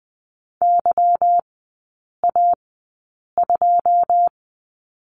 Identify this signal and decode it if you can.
Morse code, heard as a keyed tone, about 20 wpm, spaced as for 8 wpm, 700 Hz. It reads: YA2
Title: 184 Lesson 30e - Letter b - Three Characters 20wpm